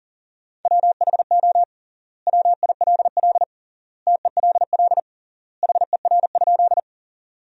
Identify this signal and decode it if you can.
Morse code, heard as a keyed tone, about 40 wpm, 700 Hz. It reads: WHO WILL TELL HER?
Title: 3x Word-Spacing - Sentences from Top 200 Words - 40wpm